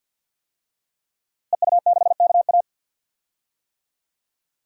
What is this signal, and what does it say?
Morse code, heard as a keyed tone, about 51 wpm, 700 Hz. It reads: EU6KA
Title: DXCC-100 Call Signs - 5 Characters - Repeated 2x - 50wpm